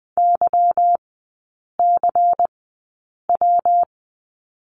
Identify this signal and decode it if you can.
Morse code, heard as a keyed tone, about 20 wpm, spaced as for 8 wpm, 700 Hz. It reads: YCW
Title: 152 Lesson 25e - Letter p - Three Characters 20wpm